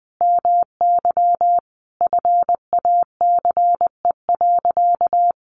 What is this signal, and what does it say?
Morse code, heard as a keyed tone, about 20 wpm, 700 Hz. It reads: MY FACE.
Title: Sentences from Top 1100 Words 20wpm